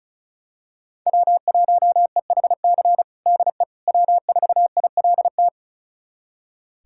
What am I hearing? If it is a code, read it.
Morse code, heard as a keyed tone, about 34 wpm, 700 Hz. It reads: W1EHC DE W4ILT
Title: USA Callsign DE Callsign - 5 Characters - 35wpm